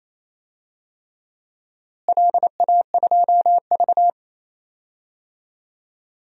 Morse code, heard as a keyed tone, about 28 wpm, 700 Hz. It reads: LA2V
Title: DXCC-100 Call Signs - 4 Characters - Repeated 3x - 28wpm